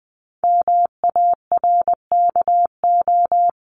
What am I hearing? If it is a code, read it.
Morse code, heard as a keyed tone, about 20 wpm, 700 Hz. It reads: MARKO
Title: CWT Exchange 20wpm